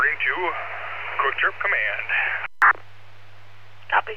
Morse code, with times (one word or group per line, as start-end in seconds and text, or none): none